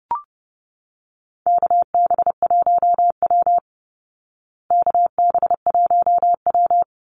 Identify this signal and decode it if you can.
Morse code, heard as a keyed tone, about 30 wpm, 700 Hz. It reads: KB1W KB1W
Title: Call Signs Repeated 3x - 30wpm